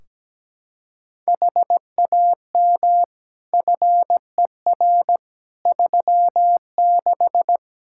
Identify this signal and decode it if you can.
Morse code, heard as a keyed tone, about 17 wpm, 700 Hz. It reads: HAM FER 36
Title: Practice QSO - Exchange 3 - More Info - 17wpm